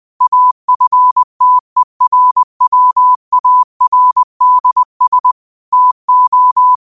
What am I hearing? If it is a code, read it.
Morse code, heard as a keyed tone, about 20 wpm, 1000 Hz. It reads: AFTERWARDS TO
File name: qotdfiles / 20wpm